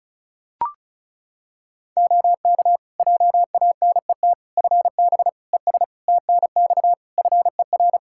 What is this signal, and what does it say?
Morse code, heard as a keyed tone, about 36 wpm, 700 Hz. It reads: OK JANET FB ES TNX FER
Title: Practice QSO - Exchange 2 - Info - 35wpm